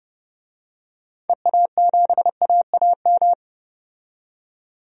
Morse code, heard as a keyed tone, about 30 wpm, 700 Hz. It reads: EA7AAM
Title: DXCC-100 Call Signs - 6 Characters - 30wpm